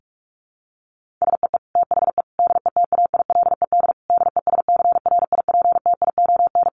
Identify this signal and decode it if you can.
Morse code, heard as a keyed tone, about 44 wpm, 700 Hz. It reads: SEE THE DETAILED DESCRIPTION
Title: Sentences from Top 1500 Words 45wpm